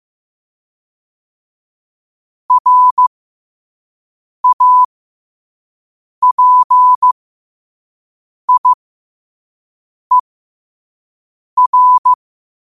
Morse code, heard as a keyed tone, about 15 wpm, spaced as for 5 wpm, 1000 Hz. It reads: RAPIER
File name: qotdfiles / 5wpm